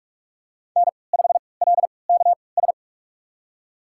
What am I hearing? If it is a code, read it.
Morse code, heard as a keyed tone, about 46 wpm, spaced as for 26 wpm, 700 Hz. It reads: N5LKS
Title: USA Call Signs - 5 Characters - 3x Character Spacing - 45wpm